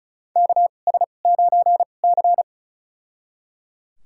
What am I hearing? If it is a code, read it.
Morse code, heard as a keyed tone, about 36 wpm, spaced as for 25 wpm, 700 Hz. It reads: KS9C
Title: USA Call Signs - 2x Character Spacing - 35wpm